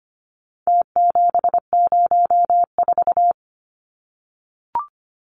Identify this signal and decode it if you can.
Morse code, heard as a keyed tone, about 25 wpm, 700 Hz. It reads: T704
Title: Numbers - Contest Style - 4-Digit 25wpm